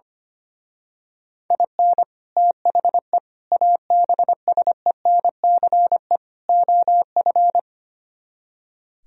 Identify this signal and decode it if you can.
Morse code, heard as a keyed tone, about 25 wpm, 700 Hz. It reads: IN THE ABSENCE OF
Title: Speed-Racing - Sets of 4 Words 25wpm